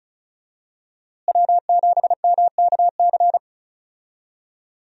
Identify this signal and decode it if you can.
Morse code, heard as a keyed tone, about 36 wpm, 700 Hz. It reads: W7MKC
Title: USA Call Signs - 5 Characters - Repeated 3x - 35wpm